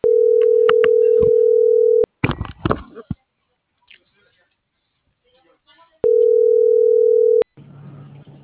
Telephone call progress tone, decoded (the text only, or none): ringback tone